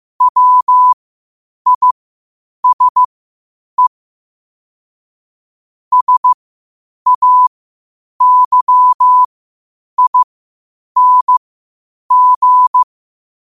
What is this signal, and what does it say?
Morse code, heard as a keyed tone, about 15 wpm, spaced as for 8 wpm, 1000 Hz. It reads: WISE SAYING
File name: qotdfiles / 8wpm